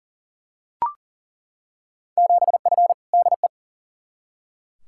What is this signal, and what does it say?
Morse code, heard as a keyed tone, about 40 wpm, 700 Hz. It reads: 7F DE
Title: Field Day Exchanges Encore - 40wpm